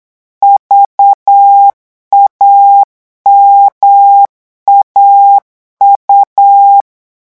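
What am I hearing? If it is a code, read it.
Morse code, heard as a keyed tone, about 8 wpm, 800 Hz. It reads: VAMAU
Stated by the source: Morse code revision lesson at 8 words per minute, from a French amateur radio course.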